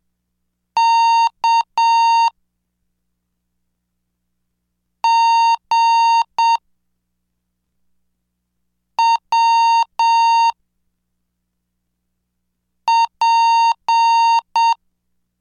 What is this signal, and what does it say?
Morse code, heard as a keyed tone, about 7 wpm, 920 Hz. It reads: K G W P
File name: cursos / CDestudo / 16 Track 16